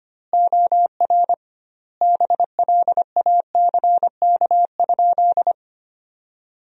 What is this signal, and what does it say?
Morse code, heard as a keyed tone, about 25 wpm, 700 Hz. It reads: OR BLACK?
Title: 2x Word-Spacing - Sentences from Top 400 Words - 25wpm